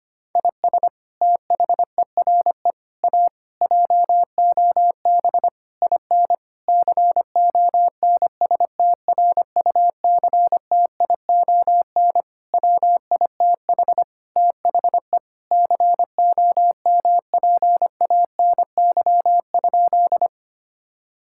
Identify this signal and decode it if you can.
Morse code, heard as a keyed tone, about 25 wpm, 700 Hz. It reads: IS THERE A JOB IN CONSTRUCTION WITH THE COMPANY?
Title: Sentences from Top 1600 Words 25wpm